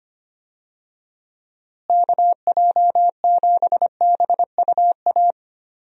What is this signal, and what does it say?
Morse code, heard as a keyed tone, about 25 wpm, 700 Hz. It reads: KJ7BUA
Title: USA Call Signs - 6 Characters - Repeated 3x - 25wpm